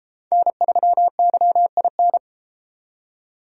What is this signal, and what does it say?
Morse code, heard as a keyed tone, about 29 wpm, 700 Hz. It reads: N3YIN